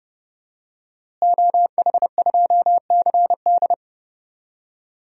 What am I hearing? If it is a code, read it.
Morse code, heard as a keyed tone, about 32 wpm, 700 Hz. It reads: OH2CD
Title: DXCC-100 Call Signs - 5 Characters - Repeated 2x - 30wpm